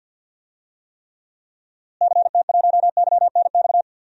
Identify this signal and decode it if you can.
Morse code, heard as a keyed tone, about 49 wpm, 700 Hz. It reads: CT1YNX
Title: DXCC-100 Call Signs - 6 Characters - Repeated 3x - 50wpm